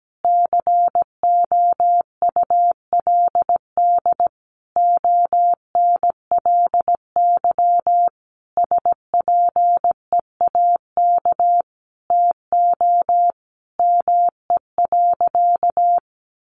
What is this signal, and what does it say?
Morse code, heard as a keyed tone, about 17 wpm, 700 Hz. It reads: COULD ONLY SPEAK TO ME.
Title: Sentences from Top 1100 Words 17wpm